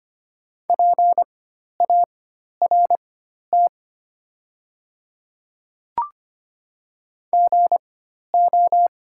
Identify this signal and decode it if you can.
Morse code, heard as a keyed tone, about 25 wpm, spaced as for 12 wpm, 690 Hz. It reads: PART GO
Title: 4x Character Spacing - Top 500 Words - 25wpm